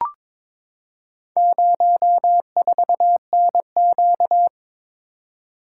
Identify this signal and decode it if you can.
Morse code, heard as a keyed tone, about 23 wpm, 700 Hz. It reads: 04NQ
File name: practice / 22wpm / Four Letter-Number Combinations - 22wpm